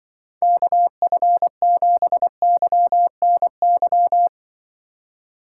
Morse code, heard as a keyed tone, about 23 wpm, 700 Hz. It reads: KF7YNY
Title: Speed Racing - USA 2x3 Call Signs 20wpm